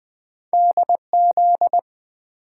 Morse code, heard as a keyed tone, about 20 wpm, 700 Hz. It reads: DZ
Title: Two Letter-Number Combinations - 20wpm